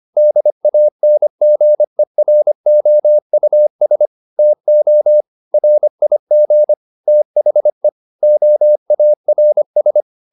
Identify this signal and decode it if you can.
Morse code, heard as a keyed tone, about 25 wpm, 600 Hz. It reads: DANGEROUS TO RIG THE OARS